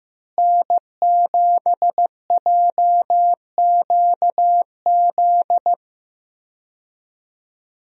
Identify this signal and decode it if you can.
Morse code, heard as a keyed tone, about 15 wpm, 700 Hz. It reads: N7JQZ
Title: USA Call Signs - 5 Characters - Repeated 3x - 15wpm